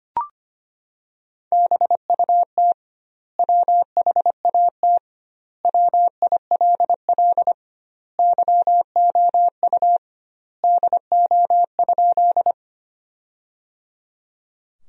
Morse code, heard as a keyed tone, about 25 wpm, 700 Hz. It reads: BUT WHAT WILL YOU DO?
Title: 2x Word-Spacing - Sentences from Top 100 Words - 25wpm